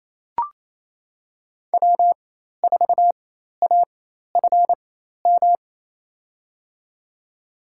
Morse code, heard as a keyed tone, about 28 wpm, spaced as for 13 wpm, 690 Hz. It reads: W4AFM
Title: USA Call Signs - 5 Characters - 4x Character Spacing - 28wpm